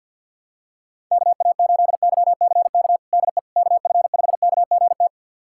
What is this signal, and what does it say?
Morse code, heard as a keyed tone, about 47 wpm, 700 Hz. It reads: KA8YCK DE KF5KGT
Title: USA Callsign DE Callsign - 6 Characters - 50wpm